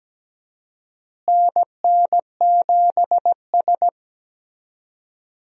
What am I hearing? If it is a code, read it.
Morse code, heard as a keyed tone, about 17 wpm, 700 Hz. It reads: NN7S
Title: USA Call Signs - 4 Characters - Repeated 2x - 17wpm